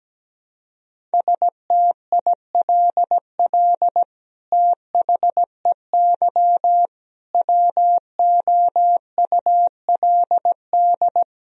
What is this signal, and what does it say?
Morse code, heard as a keyed tone, about 17 wpm, 700 Hz. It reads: STILL THEY WOULD
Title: Sentences from Top 1000 Words - Review Part 1 - 17wpm